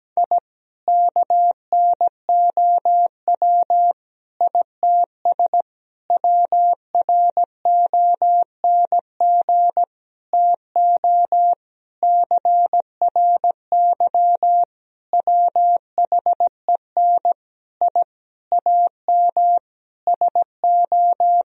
Morse code, heard as a keyed tone, about 17 wpm, 700 Hz. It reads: I KNOW ITS WRONG TO CRY WHEN I AM SO
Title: Sentences from Top 1000 Words - Review Part 2 - 17wpm